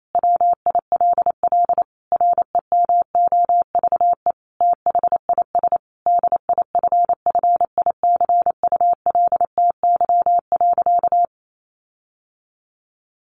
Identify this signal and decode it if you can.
Morse code, heard as a keyed tone, about 28 wpm, 700 Hz. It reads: WILL REMOVE THIS DIFFICULTY.
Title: Sentences from Top 1500 Words 28wpm